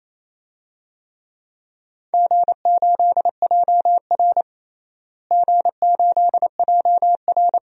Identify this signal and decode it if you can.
Morse code, heard as a keyed tone, about 28 wpm, 700 Hz. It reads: G8JR G8JR
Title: DXCC-100 Call Signs - 4 Characters - Repeated 2x - 28wpm